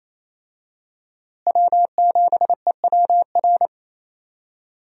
Morse code, heard as a keyed tone, about 28 wpm, 700 Hz. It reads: W7EWR
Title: USA Call Signs - 5 Characters - Repeated 2x - 28wpm